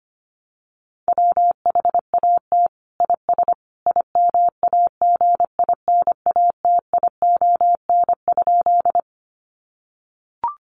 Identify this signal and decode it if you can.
Morse code, heard as a keyed tone, about 25 wpm, 700 Hz. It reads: WHAT IS IMAGINATION?
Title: Sentences from Top 1300 Words 25wpm